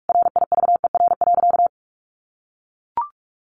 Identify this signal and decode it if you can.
Morse code, heard as a keyed tone, about 44 wpm, 700 Hz. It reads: RIVER.
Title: Sentences from Top 1400 Words 45wpm